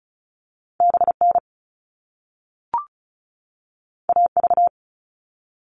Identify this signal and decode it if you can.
Morse code, heard as a keyed tone, about 36 wpm, 690 Hz. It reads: BN AV
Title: All Cut Numbers - 2-Digit 35wpm